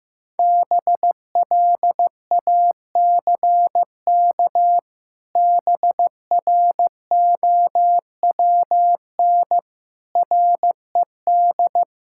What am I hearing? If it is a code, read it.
Morse code, heard as a keyed tone, about 14 wpm, 700 Hz. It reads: BLACK BROWN RED
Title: Pangram Challenge v2 - 15wpm